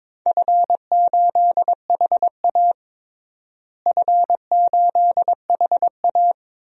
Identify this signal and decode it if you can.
Morse code, heard as a keyed tone, about 22 wpm, 700 Hz. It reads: F8HA F8HA
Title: DXCC-100 Call Signs - 4 Characters - Repeated 2x - 22wpm